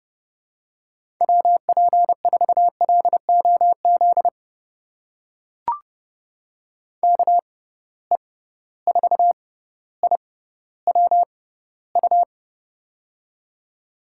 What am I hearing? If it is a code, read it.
Morse code, heard as a keyed tone, about 30 wpm, spaced as for 15 wpm, 700 Hz. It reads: WP4LOZ KE4IWU
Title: USA Call Signs - 6 Characters - 6x Character Spacing - 30wpm